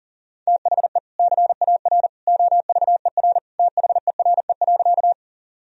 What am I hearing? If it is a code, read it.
Morse code, heard as a keyed tone, about 40 wpm, 700 Hz. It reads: THE CAR OVER THERE.
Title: Sentences from Top 1000 Words - Review Part 1 - 40wpm